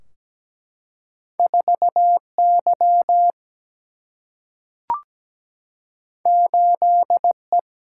Morse code, heard as a keyed tone, about 17 wpm, 700 Hz. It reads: 4Y 8E
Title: Single-Letter Single-Number 17wpm